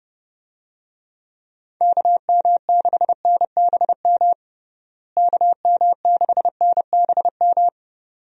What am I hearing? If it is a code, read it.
Morse code, heard as a keyed tone, about 30 wpm, 700 Hz. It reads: KM6NBM KM6NBM
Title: USA Call Signs - 6 Characters - Repeated 2x - 30wpm